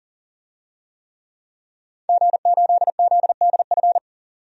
Morse code, heard as a keyed tone, about 39 wpm, 700 Hz. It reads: G8ZDF
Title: DXCC-100 Call Signs - 5 Characters - 40wpm